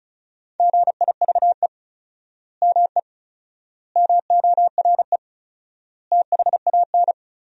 Morse code, heard as a keyed tone, about 36 wpm, 700 Hz. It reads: GIVE ME MORE THAN
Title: 4x Word-Spacing - Sentences from Top 200 Words - 35wpm